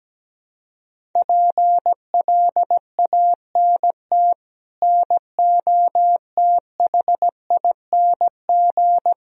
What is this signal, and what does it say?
Morse code, heard as a keyed tone, about 17 wpm, 700 Hz. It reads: PLANT NOTHING
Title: Sentences from Top 1000 Words - Review Part 1 - 17wpm